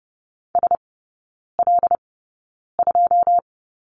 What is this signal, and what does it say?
Morse code, heard as a keyed tone, about 30 wpm, spaced as for 9 wpm, 700 Hz. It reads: SL2
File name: 081 Lesson 14e - Number 2 - Three Characters 30wpm